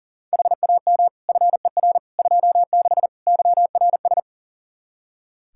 Morse code, heard as a keyed tone, about 40 wpm, 700 Hz. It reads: HAM FER 26 YRS
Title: Practice QSO - Exchange 3 - More Info - 40wpm